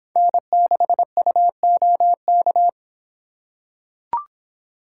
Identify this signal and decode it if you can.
Morse code, heard as a keyed tone, about 24 wpm, 700 Hz. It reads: N6UOK